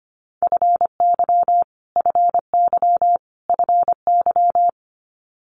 Morse code, heard as a keyed tone, about 25 wpm, 700 Hz. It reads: FY FY FY
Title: Multi-letter Phonemes Repeated 3x 25wpm